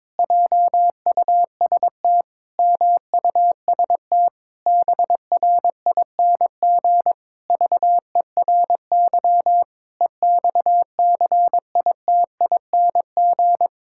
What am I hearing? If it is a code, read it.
Morse code, heard as a keyed tone, about 22 wpm, 700 Hz. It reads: JUST MUST BRING VERY EXCITING
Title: Pangram Challenge v2 - 22wpm